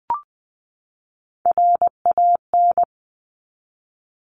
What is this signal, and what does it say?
Morse code, heard as a keyed tone, about 20 wpm, 700 Hz. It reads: RAN
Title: Top 500 Words 20wpm